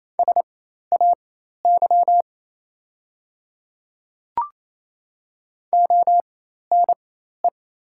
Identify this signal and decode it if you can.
Morse code, heard as a keyed tone, about 27 wpm, spaced as for 13 wpm, 690 Hz. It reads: SAY ONE
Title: 4x Character Spacing - Top 300 Words - 28wpm